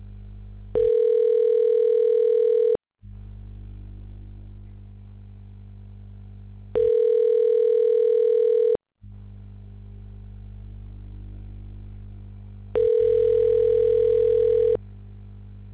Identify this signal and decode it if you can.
Ringback tone (North America), 440 + 480 Hz.